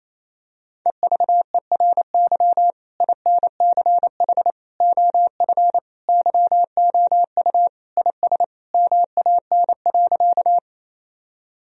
Morse code, heard as a keyed tone, about 28 wpm, 700 Hz. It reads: EVERY INCH OF YOU IS MAN.
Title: Sentences from Top 1000 Words - Review Part 3 - 28wpm